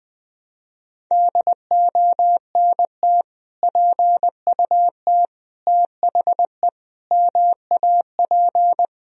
Morse code, heard as a keyed tone, about 20 wpm, 700 Hz. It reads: DONT PUT THE MAP
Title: Sentences from Top 1000 Words - Review Part 2 - 20wpm